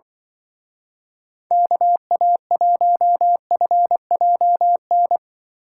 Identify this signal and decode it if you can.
Morse code, heard as a keyed tone, about 23 wpm, 700 Hz. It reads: KA1FJN
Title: Speed Racing - USA 2x3 Call Signs 20wpm